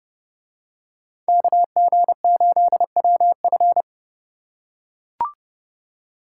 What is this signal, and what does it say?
Morse code, heard as a keyed tone, about 27 wpm, 700 Hz. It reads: KG8WF